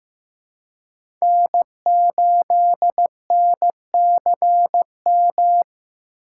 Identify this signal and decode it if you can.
Morse code, heard as a keyed tone, about 15 wpm, 700 Hz. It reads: N8NCM
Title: USA Call Signs - 5 Characters - 4x Character Spacing - 15wpm